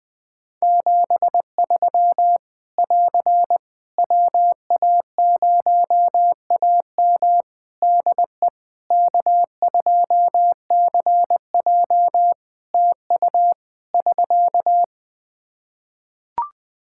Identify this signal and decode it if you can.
Morse code, heard as a keyed tone, about 20 wpm, 700 Hz. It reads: 73 <AR> WA0AM DE K2CJ TU <SK>
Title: K1USN SST QSO Practice 20wpm